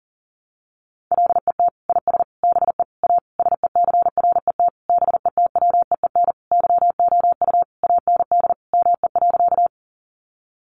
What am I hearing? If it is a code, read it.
Morse code, heard as a keyed tone, about 40 wpm, 700 Hz. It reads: LET IS BE A SECRET BETWEEN YOU AND ME.
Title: Sentences from Top 1200 Words 40wpm